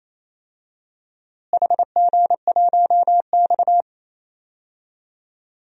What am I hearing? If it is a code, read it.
Morse code, heard as a keyed tone, about 28 wpm, 700 Hz. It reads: HG1X
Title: DXCC-100 Call Signs - 4 Characters - Repeated 2x - 28wpm